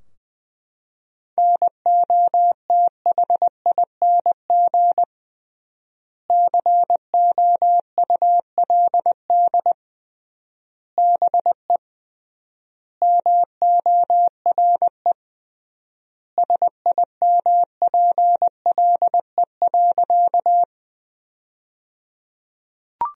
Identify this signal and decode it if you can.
Morse code, heard as a keyed tone, about 20 wpm, 700 Hz. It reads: NOTHING COULD BE MORE SIMPLE.
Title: 3x Word-Spacing - Sentences from Top 500 Words - 20wpm